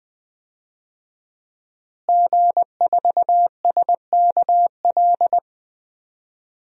Morse code, heard as a keyed tone, about 20 wpm, 700 Hz. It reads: G4SKL
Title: DXCC-100 Call Signs - 5 Characters - Repeated 3x - 20wpm